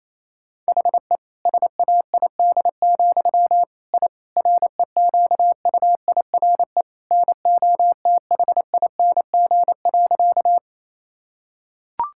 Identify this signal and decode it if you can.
Morse code, heard as a keyed tone, about 28 wpm, 700 Hz. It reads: HE SAID, I REQUIRE NOTHING.
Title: Sentences from Top 1000 Words - Review Part 6 - 28wpm